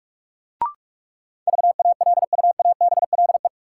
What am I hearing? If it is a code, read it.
Morse code, heard as a keyed tone, about 42 wpm, 700 Hz. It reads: VALUABLE
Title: Sentences from Top 1100 Words - New Word Drill Alt - 45wpm